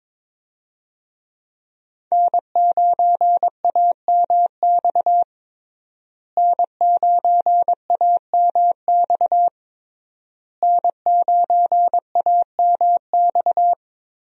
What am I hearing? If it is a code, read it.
Morse code, heard as a keyed tone, about 22 wpm, 700 Hz. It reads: N9AMX N9AMX N9AMX
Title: USA Call Signs - 5 Characters - Repeated 3x - 22wpm